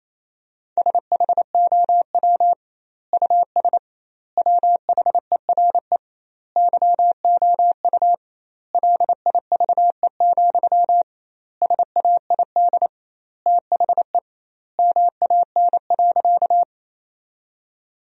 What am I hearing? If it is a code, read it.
Morse code, heard as a keyed tone, about 28 wpm, 700 Hz. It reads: SHOW US WHERE YOU LIVE, SAID THE MAN.
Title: 2x Word-Spacing - Sentences from Top 200 Words - 28wpm